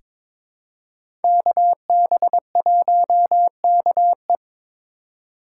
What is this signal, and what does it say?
Morse code, heard as a keyed tone, about 22 wpm, 700 Hz. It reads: KB1KE
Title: USA Call Signs - 5 Characters - 6x Character Spacing - 22wpm